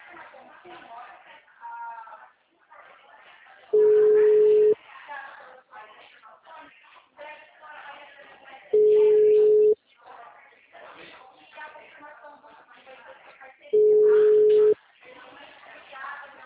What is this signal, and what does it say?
Ringback tone (Europe, 425 Hz), with speech.